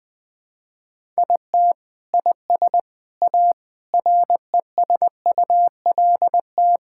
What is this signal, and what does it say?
Morse code, heard as a keyed tone, about 19 wpm, 700 Hz. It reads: IT IS A RESULT
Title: Sentences with 4-Word Phrases - 20wpm